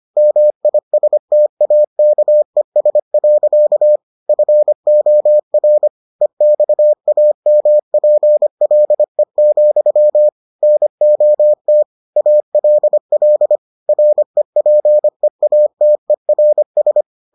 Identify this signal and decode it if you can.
Morse code, heard as a keyed tone, about 25 wpm, 600 Hz. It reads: MISTAKES. FOR EXAMPLE, NOT ALL REPEATERS